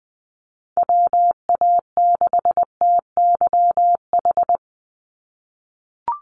Morse code, heard as a keyed tone, about 20 wpm, 700 Hz. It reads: WA6TYH
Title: USA Call Signs 20wpm